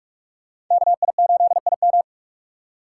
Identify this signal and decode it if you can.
Morse code, heard as a keyed tone, about 46 wpm, 700 Hz. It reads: KI8IM
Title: USA Call Signs - 5 Characters - 2x Character Spacing - 45wpm